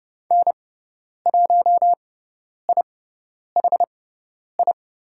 Morse code, heard as a keyed tone, about 30 wpm, spaced as for 10 wpm, 700 Hz. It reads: N1IHI
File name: course / 30:10wpm / 102 Lesson 17g - Letter u - Call Signs 30wpm